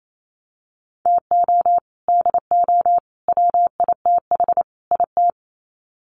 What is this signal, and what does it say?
Morse code, heard as a keyed tone, about 28 wpm, 700 Hz. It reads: TO DO WITH IT
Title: Sets of 4 Words 28wpm